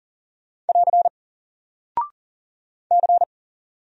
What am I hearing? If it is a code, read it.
Morse code, heard as a keyed tone, about 39 wpm, 700 Hz. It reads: <AR> C
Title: Single Character (Letter:Number:Punctuation:Prosign) 40wpm